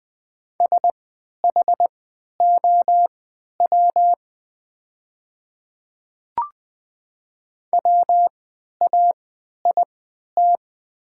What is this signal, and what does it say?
Morse code, heard as a keyed tone, about 21 wpm, spaced as for 12 wpm, 700 Hz. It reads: SHOW WAIT
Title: 3x Character Spacing - Top 500 Words - 20wpm